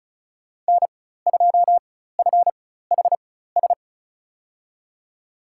Morse code, heard as a keyed tone, about 36 wpm, spaced as for 16 wpm, 700 Hz. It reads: N2FHS
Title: USA Call Signs - 5 Characters - 4x Character Spacing - 35wpm